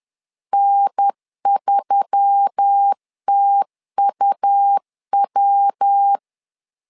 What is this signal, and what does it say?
Morse code, heard as a keyed tone, about 10 wpm, 800 Hz. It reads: N3TUW
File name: LEZIONI / 4 Q1S3 10wpm 2427